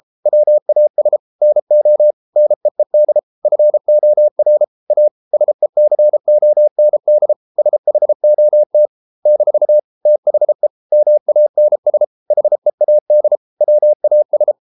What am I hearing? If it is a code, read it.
Morse code, heard as a keyed tone, about 33 wpm, 600 Hz. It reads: WAS NO NEED FOR A SECOND SHOT - THE MANS HEAD WAS